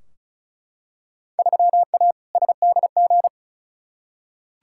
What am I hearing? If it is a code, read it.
Morse code, heard as a keyed tone, about 36 wpm, 700 Hz. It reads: 3A SDG
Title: Field Day Exchanges - 35wpm